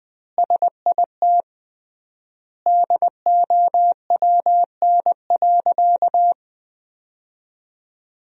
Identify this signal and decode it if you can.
Morse code, heard as a keyed tone, about 20 wpm, 700 Hz. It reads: SIT DOWN.
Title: 3x Word-Spacing - Sentences from Top 500 Words - 20wpm